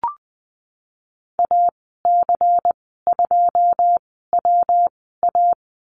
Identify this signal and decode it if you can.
Morse code, heard as a keyed tone, about 20 wpm, spaced as for 15 wpm, 700 Hz. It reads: AC2WA
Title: 140 Lesson 23g - Letter f - Call Signs 20wpm